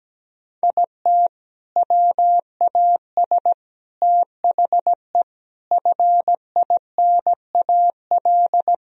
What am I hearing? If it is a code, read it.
Morse code, heard as a keyed tone, about 17 wpm, 700 Hz. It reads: IT WAS THE FINAL
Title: Sentences from Top 1000 Words - Review Part 2 - 17wpm